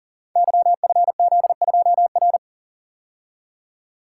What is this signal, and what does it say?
Morse code, heard as a keyed tone, about 40 wpm, 700 Hz. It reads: YFZ2R
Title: Code Groups - 5 Letter:Numbers - 40wpm